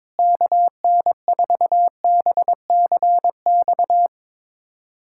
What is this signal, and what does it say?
Morse code, heard as a keyed tone, about 22 wpm, 700 Hz. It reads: KN4BCX
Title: USA Callsign DE Callsign 22wpm